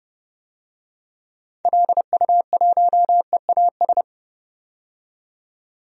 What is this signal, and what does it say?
Morse code, heard as a keyed tone, about 30 wpm, 700 Hz. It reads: LU1EAS
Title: DXCC-100 Call Signs - 6 Characters - 30wpm